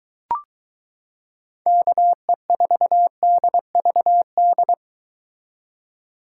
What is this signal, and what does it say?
Morse code, heard as a keyed tone, about 23 wpm, 700 Hz. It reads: KE4DVD